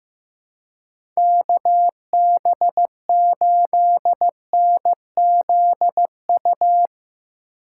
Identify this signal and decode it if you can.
Morse code, heard as a keyed tone, about 15 wpm, 700 Hz. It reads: KB8NZU
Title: USA Call Signs - 6 Characters - 3x Character Spacing - 15wpm